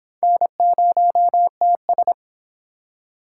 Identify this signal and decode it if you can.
Morse code, heard as a keyed tone, about 26 wpm, 700 Hz. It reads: N0TS